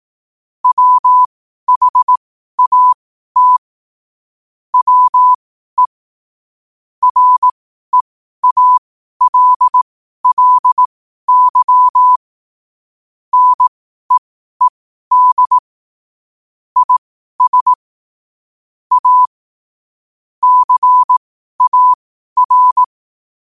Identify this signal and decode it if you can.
Morse code, heard as a keyed tone, about 18 wpm, spaced as for 12 wpm, 1000 Hz. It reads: WHAT WE REALLY NEED IS A CAR
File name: qotdfiles / 12wpm